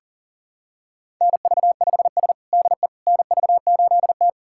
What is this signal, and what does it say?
Morse code, heard as a keyed tone, about 40 wpm, 700 Hz. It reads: NV5S DE NV8T
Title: USA Callsign DE Callsign - 4 Characters - 40wpm